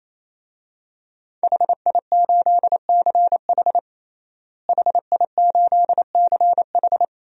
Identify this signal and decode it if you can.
Morse code, heard as a keyed tone, about 28 wpm, 700 Hz. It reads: HI8CH HI8CH
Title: DXCC-100 Call Signs - 5 Characters - Repeated 3x - 28wpm